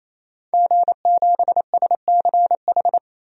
Morse code, heard as a keyed tone, about 28 wpm, 700 Hz. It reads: G7SCH
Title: DXCC-100 Call Signs - 5 Characters - 28wpm